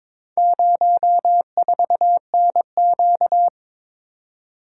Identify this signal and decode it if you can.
Morse code, heard as a keyed tone, about 23 wpm, 700 Hz. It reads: 04NQ
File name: practice / 22wpm / Four Letter-Number Combinations - 22wpm